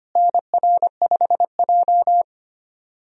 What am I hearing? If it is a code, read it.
Morse code, heard as a keyed tone, about 25 wpm, 700 Hz. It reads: NR5J
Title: USA Call Signs - 4 Characters - Repeated 2x - 25wpm